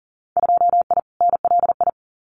Morse code, heard as a keyed tone, about 40 wpm, 700 Hz. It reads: 2I NLI
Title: Winter Field Day Exchanges 40wpm